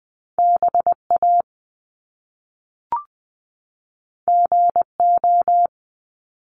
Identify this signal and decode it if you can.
Morse code, heard as a keyed tone, about 20 wpm, 700 Hz. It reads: BA GO
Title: Top Two Letter Combinations in Words 20wpm